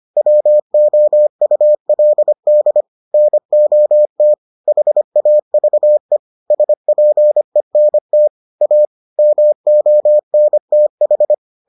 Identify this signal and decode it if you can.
Morse code, heard as a keyed tone, about 25 wpm, 600 Hz. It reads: WOULD NOT HAVE SPENT A MONTH